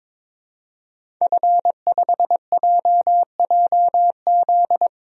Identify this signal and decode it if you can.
Morse code, heard as a keyed tone, about 22 wpm, 700 Hz. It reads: F5JJZ
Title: DXCC-100 Call Signs - 5 Characters - Repeated 3x - 22wpm